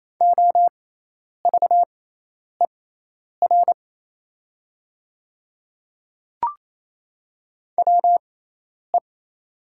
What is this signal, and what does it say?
Morse code, heard as a keyed tone, about 28 wpm, spaced as for 10 wpm, 700 Hz. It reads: OVER WE
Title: 6x Character Spacing - Top 100 Words - 28wpm